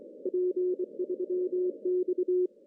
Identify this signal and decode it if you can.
Morse code, heard as a keyed tone, about 27 wpm, 360 Hz. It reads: P3X
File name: CW / cqww-cw-k5tr